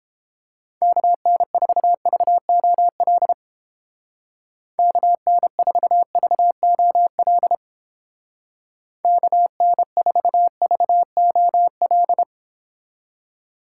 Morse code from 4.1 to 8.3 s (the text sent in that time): KN4VOL